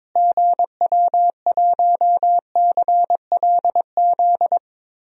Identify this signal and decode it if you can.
Morse code, heard as a keyed tone, about 22 wpm, 700 Hz. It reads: GW1CLZ
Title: DXCC-100 Call Signs - 6 Characters - Repeated 3x - 22wpm